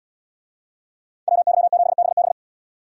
Morse code, heard as a keyed tone, about 77 wpm, 700 Hz. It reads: WY7BB